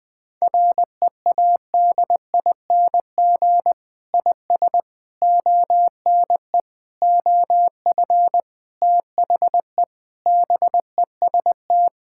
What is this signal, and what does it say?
Morse code, heard as a keyed tone, about 20 wpm, 700 Hz. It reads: READING IS ONE OF THE BEST
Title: Sentences with 6-Word Phrases - 20wpm